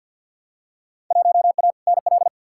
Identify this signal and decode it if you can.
Morse code, heard as a keyed tone, about 53 wpm, 700 Hz. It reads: 1A NL
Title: Field Day Exchanges Encore - 50wpm